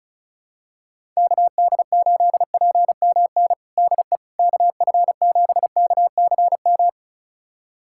Morse code, heard as a keyed tone, about 34 wpm, 700 Hz. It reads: KD8PMN DE KF7KCM
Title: USA Callsign DE Callsign - 6 Characters - 35wpm